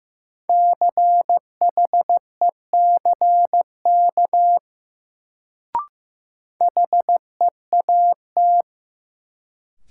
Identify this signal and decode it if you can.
Morse code, heard as a keyed tone, about 15 wpm, 700 Hz. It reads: CHECK HEAT
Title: Sentences from Top 1000 Words - Review Part 3 - New Word Drill - 15wpm